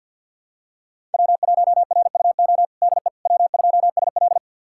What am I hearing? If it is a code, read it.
Morse code, heard as a keyed tone, about 50 wpm, 700 Hz. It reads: W1RUO DE W2SL
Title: USA Callsign DE Callsign 50wpm